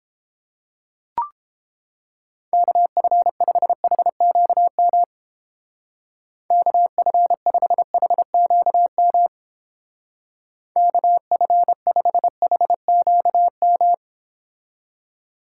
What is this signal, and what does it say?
Morse code, heard as a keyed tone, about 28 wpm, 700 Hz. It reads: KF5HQM KF5HQM KF5HQM